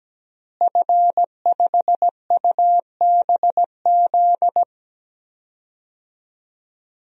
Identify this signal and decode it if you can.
Morse code, heard as a keyed tone, about 17 wpm, 700 Hz. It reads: F5UBZ
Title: DXCC-100 Call Signs - 5 Characters - Repeated 3x - 17wpm